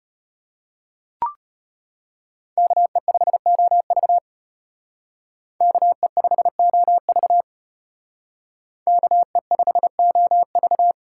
Morse code, heard as a keyed tone, about 33 wpm, 700 Hz. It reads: KE5OV KE5OV KE5OV